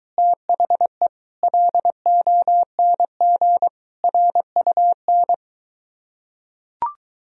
Morse code, heard as a keyed tone, about 23 wpm, 700 Hz. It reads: THE LONG RUN